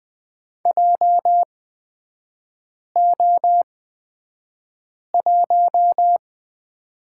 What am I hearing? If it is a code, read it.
Morse code, heard as a keyed tone, about 20 wpm, spaced as for 5 wpm, 700 Hz. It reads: JO1
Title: 224 Lesson 36e - BT - Three Characters 20wpm